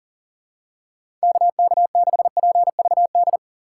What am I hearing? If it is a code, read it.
Morse code, heard as a keyed tone, about 40 wpm, 700 Hz. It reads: KK6PVD
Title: USA Call Signs - 6x Character Spacing - 40wpm